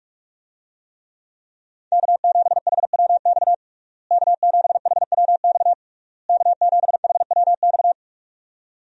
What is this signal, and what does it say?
Morse code, heard as a keyed tone, about 46 wpm, 700 Hz. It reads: K7HWX K7HWX K7HWX
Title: USA Call Signs - 5 Characters - Repeated 3x - 45wpm